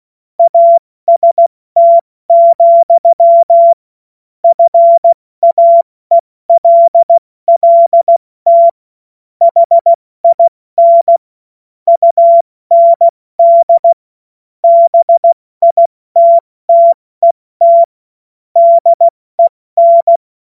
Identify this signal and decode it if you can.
Morse code, heard as a keyed tone, about 16 wpm, 680 Hz. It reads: AST, FAELLT HIN UND BITTET DEN
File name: Morsekurs / Witze / Witz 17. August 2021, 70 F80